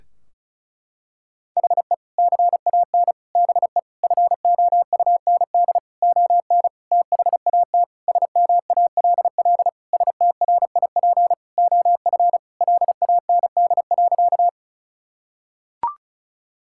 Morse code, heard as a keyed tone, about 34 wpm, 700 Hz. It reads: HE CAN BE FOUND ON THAT SMALL STRIP OF LAND.
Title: Sentences from Top 1600 Words 35wpm